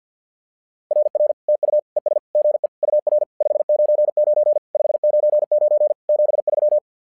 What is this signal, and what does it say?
Morse code, heard as a keyed tone, about 50 wpm, 600 Hz. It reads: RR TU ES GE UR 599 599 73
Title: SOTA QSO Practice 50wpm